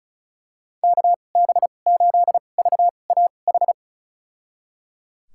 Morse code, heard as a keyed tone, about 34 wpm, spaced as for 25 wpm, 700 Hz. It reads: KB8VAH
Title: USA Call Signs - 6 Characters - 2x Character Spacing - 35wpm